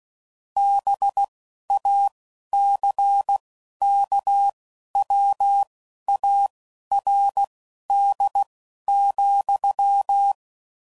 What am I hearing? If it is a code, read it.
Morse code, heard as a keyed tone, about 16 wpm, spaced as for 11 wpm, 800 Hz. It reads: BACKWARD,